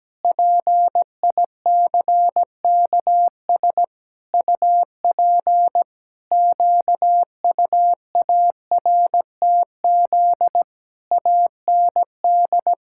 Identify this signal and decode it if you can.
Morse code, heard as a keyed tone, about 17 wpm, 700 Hz. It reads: PICKS UP QUARTZ AND
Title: Pangram Challenge v2 - 17wpm